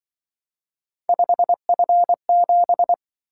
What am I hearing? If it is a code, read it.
Morse code, heard as a keyed tone, about 23 wpm, 700 Hz. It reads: 5F7
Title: Speed-Racing - Three Letter-Number Combinations - 20wpm